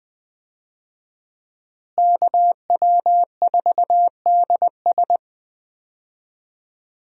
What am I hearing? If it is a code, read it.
Morse code, heard as a keyed tone, about 20 wpm, 700 Hz. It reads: KW4DS
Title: USA Call Signs - 5 Characters - Repeated 2x - 20wpm